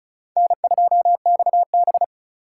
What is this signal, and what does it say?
Morse code, heard as a keyed tone, about 36 wpm, 700 Hz. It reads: N2XB
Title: USA Call Signs - 4 Characters - 4x Character Spacing - 35wpm